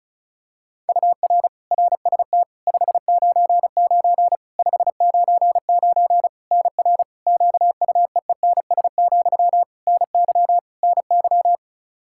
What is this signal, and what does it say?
Morse code, heard as a keyed tone, about 34 wpm, 700 Hz. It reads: UR RST 599 599 NR QUEENS, NY NY
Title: K1USN SST QSO Practice 35wpm